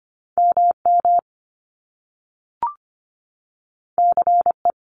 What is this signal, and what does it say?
Morse code, heard as a keyed tone, about 25 wpm, 700 Hz. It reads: MM CE
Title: Top Two Letter Combinations in Words 25wpm